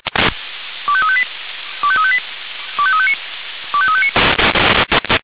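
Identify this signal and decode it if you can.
Five-tone selective call (ZVEI), heard as quick runs of five tones, tones of about 70 ms. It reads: 25358 25358 25359 25359
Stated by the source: ZVEI five-tone call received by SDR.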